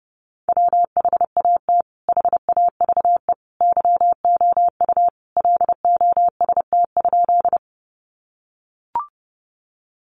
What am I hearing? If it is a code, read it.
Morse code, heard as a keyed tone, about 30 wpm, 700 Hz. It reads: WHAT HAVE YOU LOST?
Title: Sentences from Top 1400 Words 30wpm